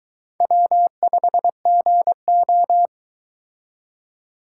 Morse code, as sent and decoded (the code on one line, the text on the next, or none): .-- ..... --. ---
W5GO